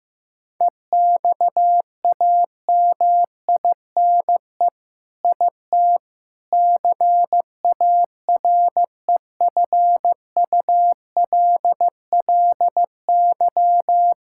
Morse code, heard as a keyed tone, about 15 wpm, 700 Hz. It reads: EXAMINE IT CAREFULLY
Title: Sentences from Top 1600 Words 15wpm